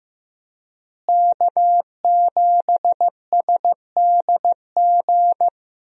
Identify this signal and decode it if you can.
Morse code, heard as a keyed tone, about 15 wpm, 700 Hz. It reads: K7SDG
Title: USA Call Signs - 5 Characters - Repeated 3x - 15wpm